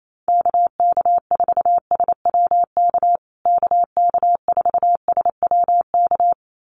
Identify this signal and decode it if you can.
Morse code, heard as a keyed tone, about 28 wpm, 700 Hz. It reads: KK4SWK KK4SWK
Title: USA Call Signs Repeated 2x 28wpm